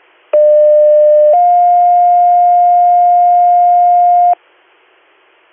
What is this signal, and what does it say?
Two-tone page, heard as a one-second tone, then a three-second tone, about 600 then 710 Hz. It is heard from 0.3 to 4.4 s.